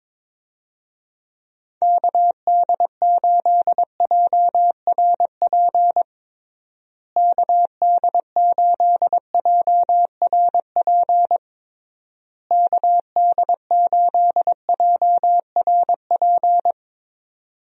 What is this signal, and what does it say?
Morse code, heard as a keyed tone, about 22 wpm, 700 Hz. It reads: KD8JRP KD8JRP KD8JRP
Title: USA Call Signs - 6 Characters - Repeated 3x - 22wpm